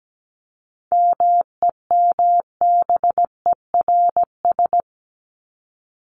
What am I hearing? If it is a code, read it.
Morse code, heard as a keyed tone, about 17 wpm, 700 Hz. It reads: MEMBERS
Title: ES:S Suffix - Top 100 Words with Base Words 17wpm